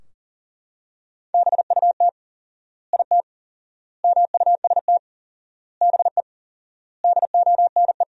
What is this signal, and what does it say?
Morse code, heard as a keyed tone, about 40 wpm, 700 Hz. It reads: BUT IT MUST BE DONE
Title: 4x Word-Spacing - Sentences from Top 500 Words - 40wpm